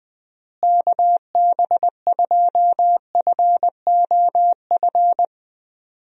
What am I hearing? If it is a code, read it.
Morse code, heard as a keyed tone, about 21 wpm, 700 Hz. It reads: KB2FOF
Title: USA Call Signs - 6 Characters - Repeated 2x - 20wpm